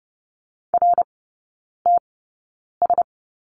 Morse code, heard as a keyed tone, about 30 wpm, spaced as for 9 wpm, 700 Hz. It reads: RTS
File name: 081 Lesson 14e - Number 2 - Three Characters 30wpm